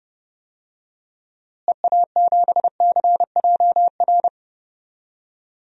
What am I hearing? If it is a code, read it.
Morse code, heard as a keyed tone, about 31 wpm, 700 Hz. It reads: EA7CJR
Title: DXCC-100 Call Signs - 6 Characters - Repeated 3x - 30wpm